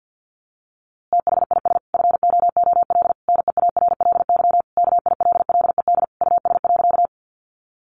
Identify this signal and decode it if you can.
Morse code, heard as a keyed tone, about 50 wpm, 700 Hz. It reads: THIS FOOL NEARLY KILLED US.
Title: A Sound Of Thunder 50wpm